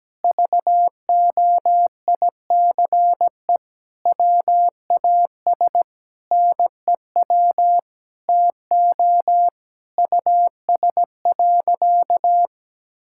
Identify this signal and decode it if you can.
Morse code, heard as a keyed tone, about 17 wpm, 700 Hz. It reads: VOICE WAS NEW TO US.
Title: Sentences from Top 1600 Words 17wpm